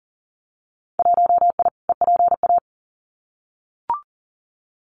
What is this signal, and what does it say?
Morse code, heard as a keyed tone, about 40 wpm, 700 Hz. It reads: 1I EPA
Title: Winter Field Day Exchanges Encore 40wpm